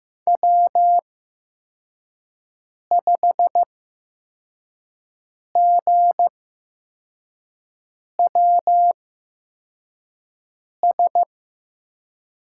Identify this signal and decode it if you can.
Morse code, heard as a keyed tone, about 15 wpm, spaced as for 4 wpm, 700 Hz. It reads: W5GWS
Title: USA Call Signs - 5 Characters - 8x Character Spacing - 15wpm